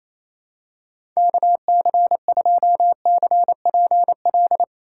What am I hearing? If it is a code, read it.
Morse code, heard as a keyed tone, about 28 wpm, 700 Hz. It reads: KC2CPL
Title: USA Call Signs - 6 Characters - 4x Character Spacing - 28wpm